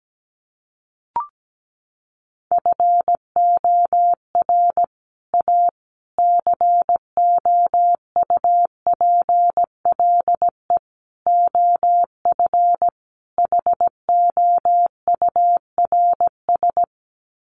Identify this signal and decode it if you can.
Morse code, heard as a keyed tone, about 17 wpm, 700 Hz. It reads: FOR A COUPLE OF HOURS
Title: Sets of 5 Words Encore2 17wpm